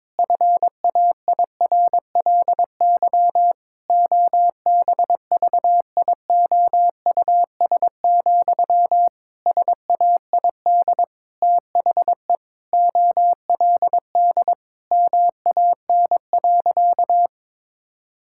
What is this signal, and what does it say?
Morse code, heard as a keyed tone, about 22 wpm, 700 Hz. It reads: FAIRLY OBVIOUS, SAID THE OLD MAN.
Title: Sentences from Top 1600 Words 22wpm